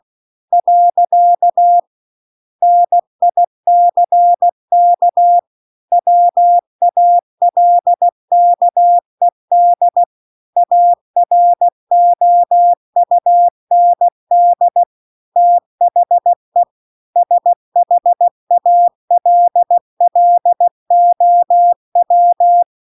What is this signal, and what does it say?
Morse code, heard as a keyed tone, about 16 wpm, 700 Hz. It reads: . NICK WALKED AROUND THE SHALLOW
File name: CW-The-Big-Twoheaded-River-2-16WPM